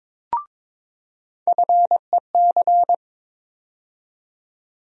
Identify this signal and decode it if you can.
Morse code, heard as a keyed tone, about 22 wpm, 700 Hz. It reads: FEC
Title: USA Call Sign Postfixes - 3 Letters 22wpm